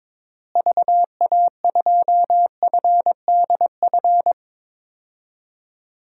Morse code, heard as a keyed tone, about 22 wpm, 700 Hz. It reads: VA2FDF
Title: DXCC-100 Call Signs - 6 Characters - Repeated 2x - 22wpm